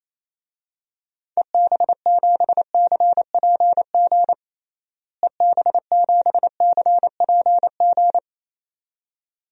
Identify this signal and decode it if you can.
Morse code, heard as a keyed tone, about 27 wpm, 700 Hz. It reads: EB7CPG EB7CPG
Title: DXCC-100 Call Signs - 6 Characters - Repeated 2x - 28wpm